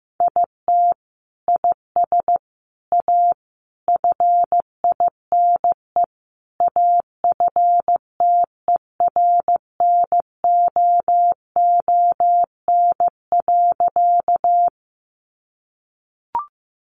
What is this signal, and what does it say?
Morse code, heard as a keyed tone, about 15 wpm, 700 Hz. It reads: IT IS A FINE AFTERNOON.
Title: Sentences from Top 1500 Words 15wpm